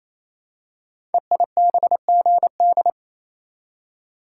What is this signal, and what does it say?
Morse code, heard as a keyed tone, about 29 wpm, 700 Hz. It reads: EIBGD
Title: Code Groups - 5 Letters - 28wpm